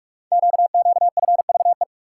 Morse code, heard as a keyed tone, about 44 wpm, 690 Hz. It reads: QQFVE
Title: Code Groups - 5 Letter:Numbers - 45wpm